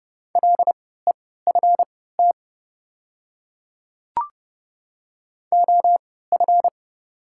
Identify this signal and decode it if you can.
Morse code, heard as a keyed tone, about 30 wpm, spaced as for 17 wpm, 690 Hz. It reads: LEFT OF
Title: 3x Character Spacing - Top 300 Words - 30wpm